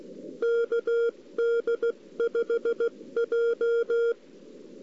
Morse code, heard as a keyed tone, about 16 wpm, 470 Hz. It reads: KD5J
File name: CW / 2008-ss-cw-k5tr-op-n5rz -02:00